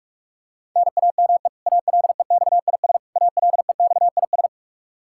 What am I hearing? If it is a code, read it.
Morse code, heard as a keyed tone, about 47 wpm, 700 Hz. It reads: NAME ALEXIS ALEXIS
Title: Practice QSO - Exchange 1 - Beginning - 45wpm